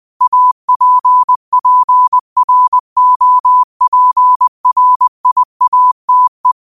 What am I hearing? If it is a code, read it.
Morse code, heard as a keyed tone, about 20 wpm, 1000 Hz. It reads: APPROPRIATE
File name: qotdfiles / 20wpm